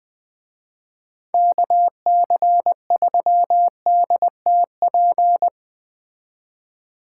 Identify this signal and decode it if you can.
Morse code, heard as a keyed tone, about 20 wpm, 700 Hz. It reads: KC3DTP
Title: Speed Racing - USA Call Signs 17wpm